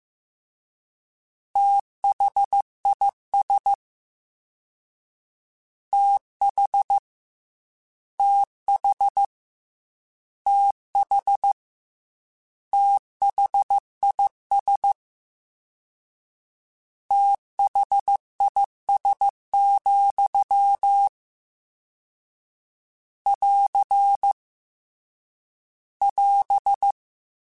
Morse code, heard as a keyed tone, about 15 wpm, 800 Hz. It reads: THIS TH TH TH THIS THIS, <AR> <AS>
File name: Combinations / Combinations-mp3s / Combinations@13